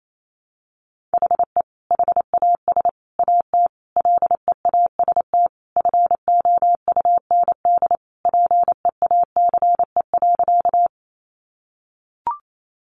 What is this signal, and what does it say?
Morse code, heard as a keyed tone, about 28 wpm, 700 Hz. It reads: HE HAS AT LEAST FOUND PEACE.
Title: Sentences from Top 1500 Words 28wpm